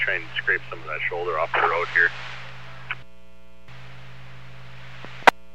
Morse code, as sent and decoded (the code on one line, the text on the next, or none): none